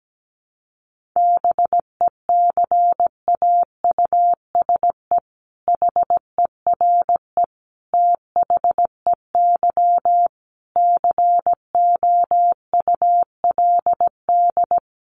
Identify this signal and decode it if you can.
Morse code, heard as a keyed tone, about 17 wpm, 700 Hz. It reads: BECAUSE HERE THEY COULD
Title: Sentences from Top 1500 Words 17wpm